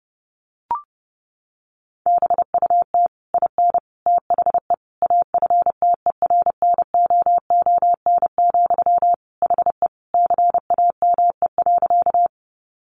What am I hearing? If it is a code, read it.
Morse code, heard as a keyed tone, about 30 wpm, 700 Hz. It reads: BUT IN THE AFTERNOON, HE CAME.
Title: Sentences from Top 1500 Words 30wpm